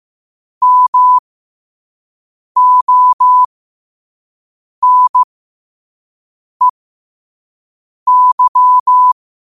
Morse code, heard as a keyed tone, about 15 wpm, spaced as for 5 wpm, 1000 Hz. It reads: MONEY